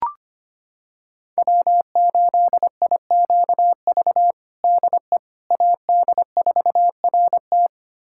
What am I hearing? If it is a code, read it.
Morse code, heard as a keyed tone, about 25 wpm, 700 Hz. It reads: W8IQV DE AD4RT
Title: USA Callsign DE Callsign 25wpm